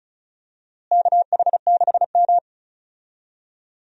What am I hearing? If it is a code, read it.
Morse code, heard as a keyed tone, about 36 wpm, 700 Hz. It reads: KH6M
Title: USA Call Signs - 4 Characters - Repeated 2x - 35wpm